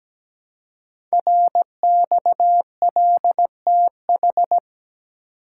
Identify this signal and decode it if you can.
Morse code, heard as a keyed tone, about 17 wpm, 700 Hz. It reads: RXLTH
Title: Code Groups - 5 Letters - 17wpm